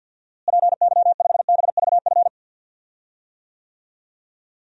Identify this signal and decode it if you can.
Morse code, heard as a keyed tone, about 53 wpm, 700 Hz. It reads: PY5BVF
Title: DXCC-100 Call Signs - 6 Characters - Repeated 3x - 50wpm